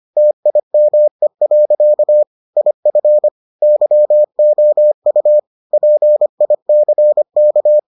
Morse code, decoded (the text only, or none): TIME. IF YOU PICK